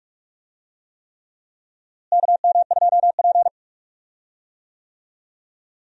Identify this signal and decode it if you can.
Morse code, heard as a keyed tone, about 44 wpm, 700 Hz. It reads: KM2P
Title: USA Call Signs - 4 Characters - Repeated 3x - 45wpm